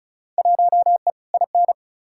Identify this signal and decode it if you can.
Morse code, heard as a keyed tone, about 36 wpm, 700 Hz. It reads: 1E IN
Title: Field Day Exchanges Encore - 35wpm